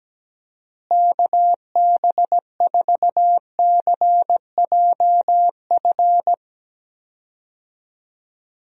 Morse code, heard as a keyed tone, about 17 wpm, 700 Hz. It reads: KB4CJF
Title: USA Call Signs - 6 Characters - Repeated 2x - 17wpm